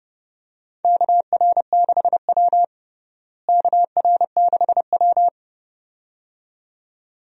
Morse code, heard as a keyed tone, about 30 wpm, 700 Hz. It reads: KR6W KR6W
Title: USA Call Signs - 4 Characters - Repeated 2x - 30wpm